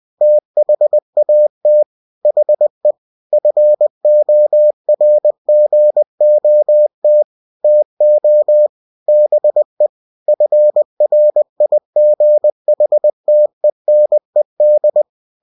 Morse code, heard as a keyed tone, about 20 wpm, 600 Hz. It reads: THAT HE FORGOT TO BE FRIGHTENED
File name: cw-books / winnie-the-pooh / winnie-20wpm-page-0062